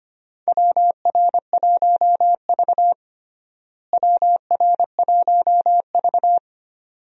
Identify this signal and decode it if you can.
Morse code, heard as a keyed tone, about 25 wpm, 700 Hz. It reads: WR1V WR1V
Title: USA Call Signs - 4 Characters - Repeated 2x - 25wpm